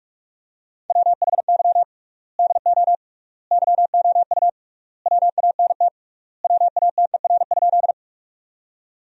Morse code, heard as a keyed tone, about 46 wpm, 700 Hz. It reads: WHY DO YOU WANT WATER?
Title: 3x Word-Spacing - Sentences from Top 200 Words - 45wpm